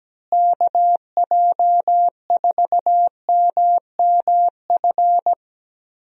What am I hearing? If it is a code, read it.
Morse code, heard as a keyed tone, about 17 wpm, 700 Hz. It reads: KJ4MMF
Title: USA Call Signs - 6 Characters - Repeated 3x - 17wpm